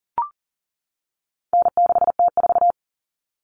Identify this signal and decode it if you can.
Morse code, heard as a keyed tone, about 40 wpm, 700 Hz. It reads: N6T4
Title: All Cut Numbers - 4-Digit 40wpm